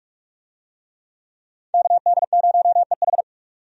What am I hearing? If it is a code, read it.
Morse code, heard as a keyed tone, about 47 wpm, 700 Hz. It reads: KD0EH
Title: USA Call Signs - 5 Characters - 45wpm